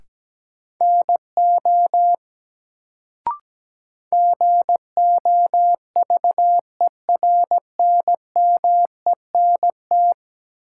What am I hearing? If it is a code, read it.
Morse code, heard as a keyed tone, about 17 wpm, 700 Hz. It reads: NO GOVERNMENT
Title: Sets of 2 Words - New Word Drill - 17wpm